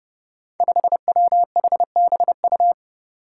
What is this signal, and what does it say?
Morse code, heard as a keyed tone, about 30 wpm, 700 Hz. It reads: 5WHBU
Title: Five Challenging Letter-Number Combinations 30wpm